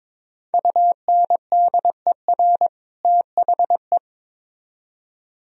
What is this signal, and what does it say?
Morse code, heard as a keyed tone, about 26 wpm, 700 Hz. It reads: UNDER THE
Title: Speed-Racing - Sets of 2 Words 22wpm